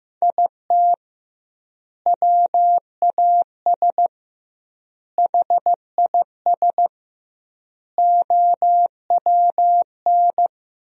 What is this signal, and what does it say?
Morse code, heard as a keyed tone, about 15 wpm, 700 Hz. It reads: IT WAS HIS OWN
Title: 2x Word-Spacing - Sentences from Top 300 Words - 15wpm